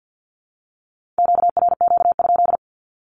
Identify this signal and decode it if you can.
Morse code, heard as a keyed tone, about 50 wpm, 700 Hz. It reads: QRQ?
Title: Q-Codes 50wpm